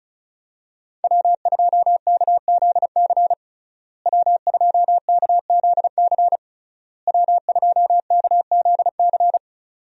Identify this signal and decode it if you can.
Morse code, heard as a keyed tone, about 36 wpm, 700 Hz. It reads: W2KZC W2KZC W2KZC
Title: USA Call Signs - 5 Characters - Repeated 3x - 35wpm